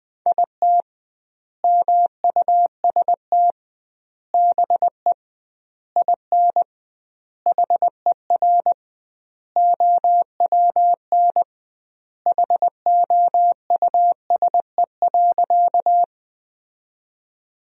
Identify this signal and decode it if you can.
Morse code, heard as a keyed tone, about 21 wpm, 700 Hz. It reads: IT MUST BE IN HER OWN HOUSE.
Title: 2x Word-Spacing - Sentences from Top 300 Words - 20wpm